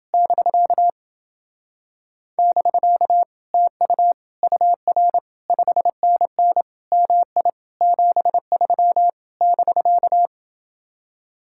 Morse code, heard as a keyed tone, about 28 wpm, 700 Hz. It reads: <BK> <BK> TU UR 5NN MI 73 <BK>